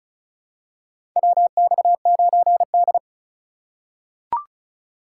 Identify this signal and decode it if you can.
Morse code, heard as a keyed tone, about 36 wpm, 700 Hz. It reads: WX9D
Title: USA Call Signs - 4 Characters - 2x Character Spacing - 35wpm